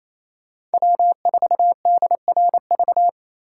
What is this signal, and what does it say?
Morse code, heard as a keyed tone, about 36 wpm, 700 Hz. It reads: W4DRV